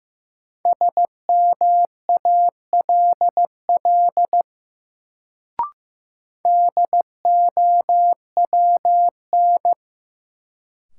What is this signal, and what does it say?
Morse code, heard as a keyed tone, about 15 wpm, 700 Hz. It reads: SMALL DOWN
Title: Sets of 2 Words - New Word Drill - 15wpm